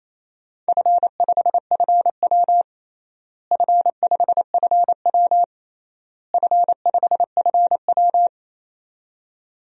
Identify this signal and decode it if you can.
Morse code, heard as a keyed tone, about 27 wpm, 700 Hz. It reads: F5FW F5FW F5FW
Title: DXCC-100 Call Signs - 4 Characters - Repeated 3x - 28wpm